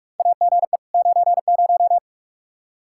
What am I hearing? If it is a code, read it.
Morse code, heard as a keyed tone, about 45 wpm, 700 Hz. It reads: AGE 90
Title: K1USN SST QSO Practice 45wpm